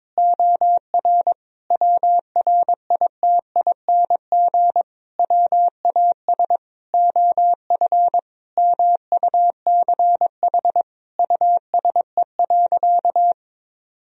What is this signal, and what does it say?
Morse code, heard as a keyed tone, about 22 wpm, 700 Hz. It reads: OR WRITING WAS OF MUCH USE.
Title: Sentences from Top 1000 Words - Review Part 3 - 22wpm